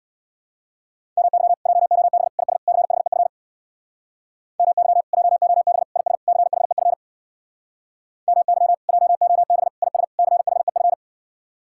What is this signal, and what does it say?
Morse code, heard as a keyed tone, about 64 wpm, 700 Hz. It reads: MY JOB IS CHEF MY JOB IS CHEF MY JOB IS CHEF